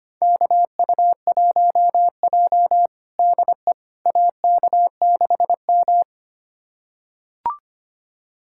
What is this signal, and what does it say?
Morse code, heard as a keyed tone, about 25 wpm, 700 Hz. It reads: KU1J DE AK6M
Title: USA Callsign DE Callsign - 4 Characters - 25wpm